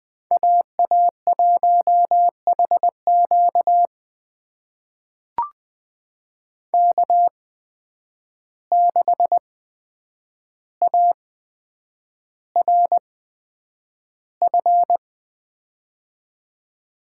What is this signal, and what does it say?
Morse code, heard as a keyed tone, about 20 wpm, spaced as for 9 wpm, 700 Hz. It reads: AA1HQ K6ARF
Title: USA Call Signs - 5 Characters - 8x Character Spacing - 20wpm